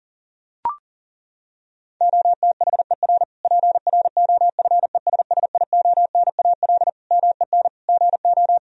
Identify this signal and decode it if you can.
Morse code, heard as a keyed tone, about 40 wpm, 700 Hz. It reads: OTHER PROFESSIONAL MEN GO
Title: Sentences from Top 1000 Words - Review Part 4 - 40wpm